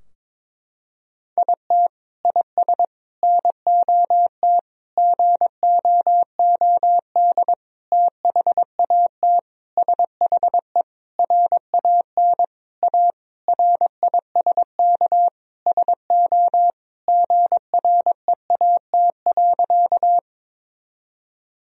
Morse code, heard as a keyed tone, about 22 wpm, 700 Hz. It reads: IT IS NOT GOOD THAT SHE RAN A RISK SO GREAT.
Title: Sentences from Top 1000 Words - Review Part 4 - 22wpm